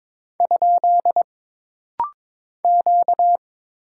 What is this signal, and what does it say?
Morse code, heard as a keyed tone, about 22 wpm, 700 Hz. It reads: ? Q
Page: This rapid-fire timing is geared to push your ICR (Instant Character Recognition) skills.